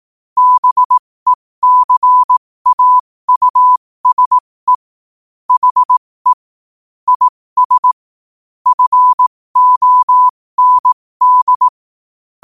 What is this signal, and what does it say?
Morse code, heard as a keyed tone, about 18 wpm, spaced as for 15 wpm, 1000 Hz. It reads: BECAUSE HE IS FOND
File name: qotdfiles / 15wpm